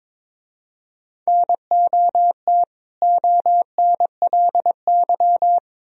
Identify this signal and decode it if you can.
Morse code, heard as a keyed tone, about 25 wpm, 700 Hz. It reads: NOT ONLY
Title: Speed-Racing - Sets of 2 Words 22wpm